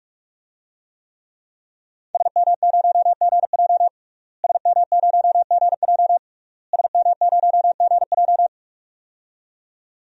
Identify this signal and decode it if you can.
Morse code, heard as a keyed tone, about 46 wpm, 700 Hz. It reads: SM0GJ SM0GJ SM0GJ
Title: DXCC-100 Call Signs - 5 Characters - Repeated 3x - 45wpm